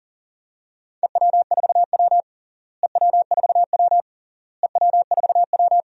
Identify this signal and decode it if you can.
Morse code, heard as a keyed tone, about 40 wpm, 700 Hz. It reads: EW4W EW4W EW4W
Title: DXCC-100 Call Signs - 4 Characters - Repeated 3x - 40wpm